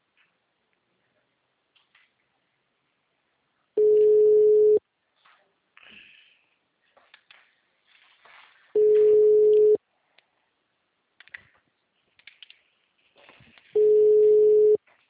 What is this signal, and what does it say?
Ringback tone (Europe, 425 Hz).